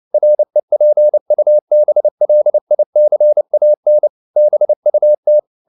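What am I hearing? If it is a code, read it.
Morse code, heard as a keyed tone, about 29 wpm, 600 Hz. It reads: REPUBLICAN BUT
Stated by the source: Morse code at 29 wpm, sending Ken Follett's novel Never in English, from CW med Gnister.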